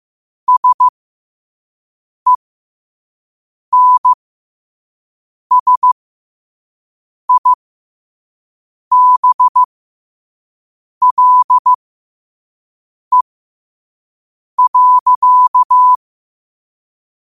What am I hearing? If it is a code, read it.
Morse code, heard as a keyed tone, about 15 wpm, spaced as for 5 wpm, 1000 Hz. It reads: SENSIBLE.
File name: qotdfiles / 5wpm